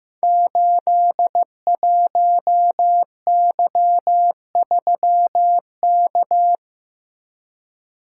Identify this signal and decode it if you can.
Morse code, heard as a keyed tone, about 15 wpm, 700 Hz. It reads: 81Y3K
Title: Code Groups - 5 Letter:Numbers - 15wpm